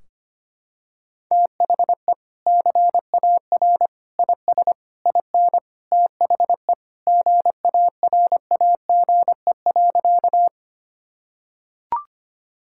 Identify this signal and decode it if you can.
Morse code, heard as a keyed tone, about 25 wpm, 700 Hz. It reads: THE CAR IS IN THE GARAGE.
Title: Sentences from Top 1600 Words 25wpm